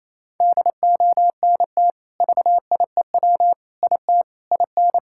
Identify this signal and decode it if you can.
Morse code, heard as a keyed tone, about 28 wpm, 700 Hz. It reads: DONT VIEW IT IN
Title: Sentences from Top 1000 Words - Review Part 6 - 28wpm